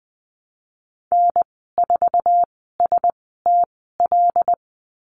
Morse code, heard as a keyed tone, about 19 wpm, spaced as for 14 wpm, 700 Hz. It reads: N4STL
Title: 076 Lesson 13g - Letter L - Call Signs 20wpm